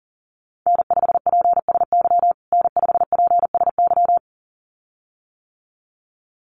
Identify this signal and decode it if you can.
Morse code, heard as a keyed tone, about 40 wpm, 700 Hz. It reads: N5PSY N5PSY
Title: USA Call Signs Repeated 2x 40wpm